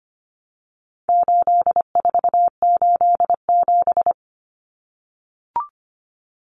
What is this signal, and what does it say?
Morse code, heard as a keyed tone, about 25 wpm, 700 Hz. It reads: 8487
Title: Common Cut Numbers - 4-Digit 25wpm